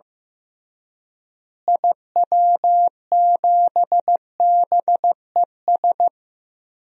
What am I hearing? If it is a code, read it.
Morse code, heard as a keyed tone, about 15 wpm, 700 Hz. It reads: IW7BES
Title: DXCC-100 Call Signs - 6 Characters - Repeated 3x - 15wpm